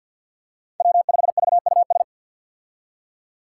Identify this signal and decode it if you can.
Morse code, heard as a keyed tone, about 51 wpm, 700 Hz. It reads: W5VUS
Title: Five Challenging Letter-Number Combinations 50wpm